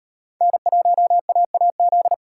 Morse code, heard as a keyed tone, about 38 wpm, 700 Hz. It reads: N1AAZ